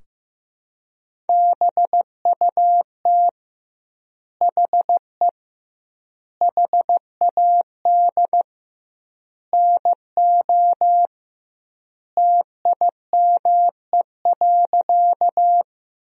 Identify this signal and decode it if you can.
Morse code, heard as a keyed tone, about 15 wpm, 700 Hz. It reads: BUT HE HAD NO TIME.
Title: 2x Word-Spacing - Sentences from Top 100 Words - 15wpm